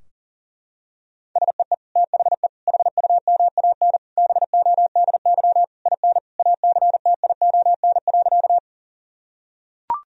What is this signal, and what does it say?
Morse code, heard as a keyed tone, about 40 wpm, 700 Hz. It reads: SEE THE HUMAN BODY IN ACTION.
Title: Sentences from Top 1000 Words - Review Part 4 - 40wpm